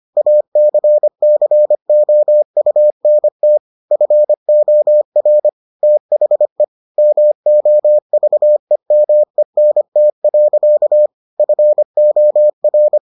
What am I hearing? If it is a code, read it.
Morse code, heard as a keyed tone, about 25 wpm, 600 Hz. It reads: ACCOUNT FOR THE MOVEMENT. FOR